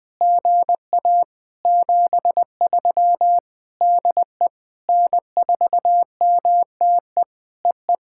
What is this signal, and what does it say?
Morse code, heard as a keyed tone, about 19 wpm, 700 Hz. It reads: GA 73 DE N4MTE EE
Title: POTA QSO Practice - 20wpm